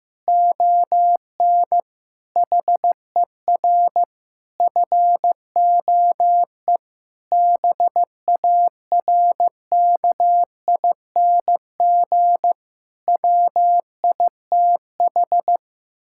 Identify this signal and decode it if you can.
Morse code, heard as a keyed tone, about 15 wpm, 700 Hz. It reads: ON HER FOE BARKING WITH
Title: Pangram Challenge v2 - 15wpm